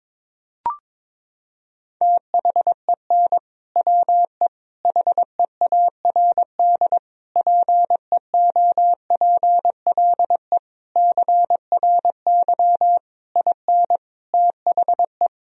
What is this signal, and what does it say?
Morse code, heard as a keyed tone, about 22 wpm, 700 Hz. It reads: THEN WE HEARD PEOPLE CRY IN THE
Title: Sentences from Top 1000 Words - Review Part 2 - 22wpm